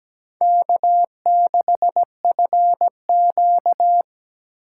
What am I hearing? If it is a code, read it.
Morse code, heard as a keyed tone, about 20 wpm, 700 Hz. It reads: K6FQ
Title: Speed Racing - USA 1x2 Call Signs 17wpm